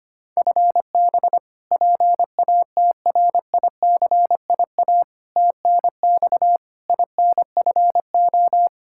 Morse code, heard as a keyed tone, about 25 wpm, 700 Hz. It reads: FB PATRICIA TNX INFO
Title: Practice QSO - Exchange 3 - More Info - 25wpm